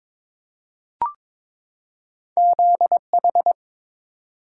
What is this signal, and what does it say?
Morse code, heard as a keyed tone, about 22 wpm, 700 Hz. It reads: ZH
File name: practice / 22wpm / Two Letter-Number Combinations - 22wpm